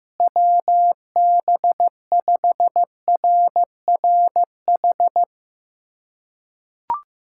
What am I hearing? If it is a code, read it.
Morse code, heard as a keyed tone, about 15 wpm, 700 Hz. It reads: WB5RRH
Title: USA Callsign DE Callsign - 6 Characters - 15wpm